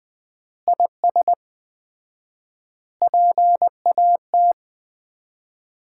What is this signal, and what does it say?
Morse code, heard as a keyed tone, about 21 wpm, 700 Hz. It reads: IS PAT
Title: Common QSO Phrases - 4x Word-Spacing - 20wpm